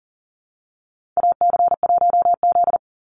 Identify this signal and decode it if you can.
Morse code, heard as a keyed tone, about 40 wpm, 700 Hz. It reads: AC1Z
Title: NA SOTA Chaser Callsigns 40wpm